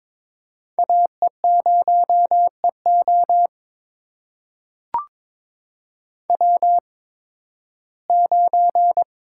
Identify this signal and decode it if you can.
Morse code, heard as a keyed tone, about 22 wpm, spaced as for 14 wpm, 700 Hz. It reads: AE0EO W9
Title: USA Call Signs - 5 Characters - 8x Character Spacing - 22wpm